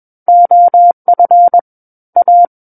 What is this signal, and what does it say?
Morse code, heard as a keyed tone, about 21 wpm, 700 Hz. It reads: OF A
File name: Book One / 12-Chapter 12 - 21 WPM